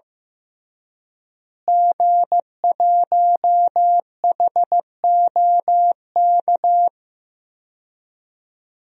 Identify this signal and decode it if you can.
Morse code, heard as a keyed tone, about 15 wpm, 700 Hz. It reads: G1HOK
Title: DXCC-100 Call Signs - 5 Characters - Repeated 3x - 15wpm